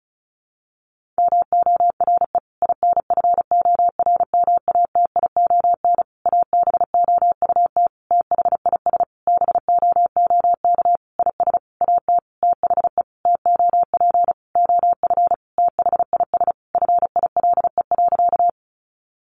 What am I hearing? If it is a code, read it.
Morse code, heard as a keyed tone, about 34 wpm, 700 Hz. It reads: MORE INFORMATION ABOUT THIS BOOK IS AT THE TOP OF THIS FILE.
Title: Sentences from Top 1200 Words 35wpm